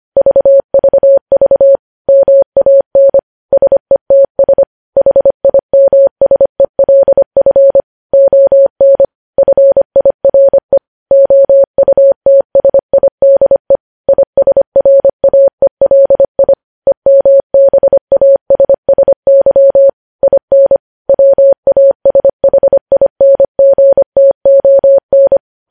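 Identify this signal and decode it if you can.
Morse code, heard as a keyed tone, about 25 wpm, 550 Hz. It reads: VVV MAN SETS HIMSELF ON FIRE OUTSIDE ISRAELI EMBASSY IN WASHINGTON
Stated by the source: News headlines sent at 25.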